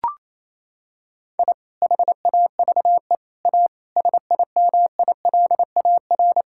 Morse code, heard as a keyed tone, about 28 wpm, 700 Hz. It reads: I HAVE A SIMILAR
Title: Sentences from Top 1000 Words - Review Part 5 - 28wpm